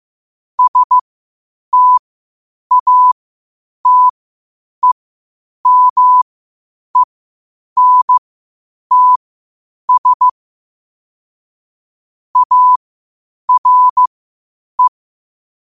Morse code, heard as a keyed tone, about 15 wpm, spaced as for 8 wpm, 1000 Hz. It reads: STATEMENTS ARE
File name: qotdfiles / 8wpm